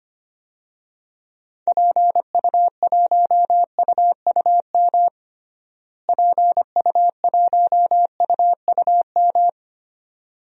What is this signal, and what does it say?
Morse code, heard as a keyed tone, about 25 wpm, 700 Hz. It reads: PU1UUM PU1UUM
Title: DXCC-100 Call Signs - 6 Characters - Repeated 2x - 25wpm